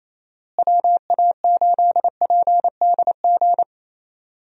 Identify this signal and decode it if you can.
Morse code, heard as a keyed tone, about 28 wpm, 700 Hz. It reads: WA8PDG
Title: USA Call Signs - 6 Characters - Repeated 3x - 28wpm